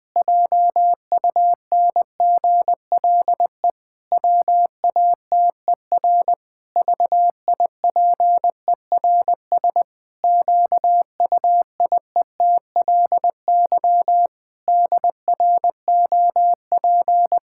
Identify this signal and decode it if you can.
Morse code, heard as a keyed tone, about 20 wpm, 700 Hz. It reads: JUNGLE WATER VIPERS QUIETLY DROP
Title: Pangram Challenge v2 - 20wpm